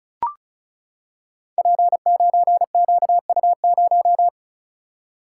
Morse code, heard as a keyed tone, about 34 wpm, 700 Hz. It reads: P9QU0
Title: Code Groups - 5 Letter:Numbers - 35wpm